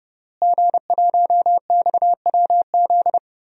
Morse code, heard as a keyed tone, about 30 wpm, 700 Hz. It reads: G1XWZ
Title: DXCC-100 Call Signs - 5 Characters - Repeated 3x - 30wpm